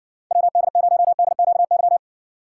Morse code, heard as a keyed tone, about 60 wpm, 700 Hz. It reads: WD0DQY